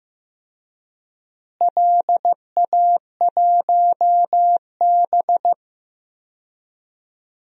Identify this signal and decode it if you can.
Morse code, heard as a keyed tone, about 15 wpm, 700 Hz. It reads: LA1B
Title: DXCC-100 Call Signs - 4 Characters - Repeated 2x - 15wpm